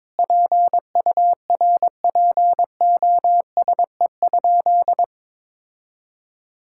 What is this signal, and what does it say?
Morse code, heard as a keyed tone, about 22 wpm, 700 Hz. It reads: PURPOSE?
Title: Sentences from Top 1000 Words - Review Part 5 - 22wpm